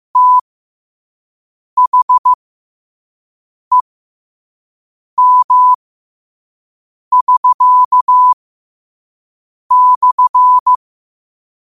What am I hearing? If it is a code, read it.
Morse code, heard as a keyed tone, about 15 wpm, spaced as for 5 wpm, 1000 Hz. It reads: THEM<SK>/
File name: qotdfiles / 5wpm